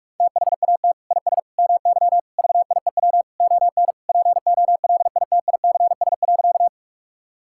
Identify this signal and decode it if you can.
Morse code, heard as a keyed tone, about 46 wpm, 700 Hz. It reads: THAT IS MY VIEW ON POLITICS.
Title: Sentences from Top 1000 Words - Review Part 6 - 45wpm